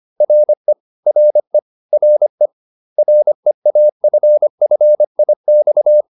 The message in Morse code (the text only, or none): RE RE RE REAFFIX